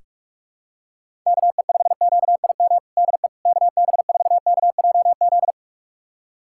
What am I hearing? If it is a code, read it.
Morse code, heard as a keyed tone, about 46 wpm, 700 Hz. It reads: KE5QIM DE KB4KJZ
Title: USA Callsign DE Callsign - 6 Characters - 45wpm